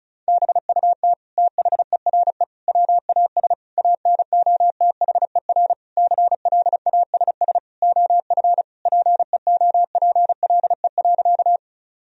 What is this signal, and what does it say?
Morse code, heard as a keyed tone, about 34 wpm, 700 Hz. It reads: BUT THERE WAS ANOTHER CLASS OF PEOPLE.
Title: Sentences from Top 1000 Words - Review Part 2 - 35wpm